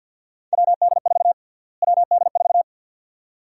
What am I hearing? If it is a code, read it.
Morse code, heard as a keyed tone, about 52 wpm, 700 Hz. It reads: WD4 WD4
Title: USA Call Sign Prefixes - Repeated 2x - 50wpm